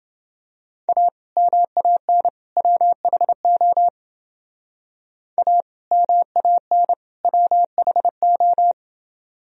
Morse code, heard as a keyed tone, about 29 wpm, 700 Hz. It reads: A MAN WHO A MAN WHO